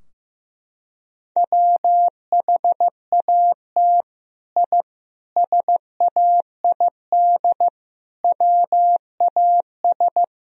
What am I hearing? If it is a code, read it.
Morse code, heard as a keyed tone, about 15 wpm, 700 Hz. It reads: WHAT I SAID WAS
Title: Sentences from Top 1000 Words - Review Part 4 - 15wpm